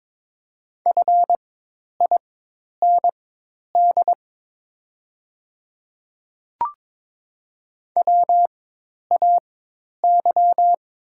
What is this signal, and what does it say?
Morse code, heard as a keyed tone, about 22 wpm, spaced as for 10 wpm, 700 Hz. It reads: FIND WAY
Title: 4x Character Spacing - Top 100 Words - 22wpm